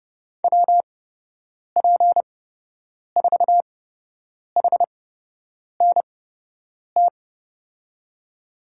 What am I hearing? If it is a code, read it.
Morse code, heard as a keyed tone, about 30 wpm, spaced as for 8 wpm, 700 Hz. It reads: WP4HNT
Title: USA Call Signs - 6 Characters - 8x Character Spacing - 30wpm